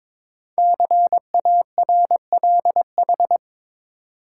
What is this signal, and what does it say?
Morse code, heard as a keyed tone, about 22 wpm, 700 Hz. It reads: CARLH
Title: Code Groups - 5 Letters - 22wpm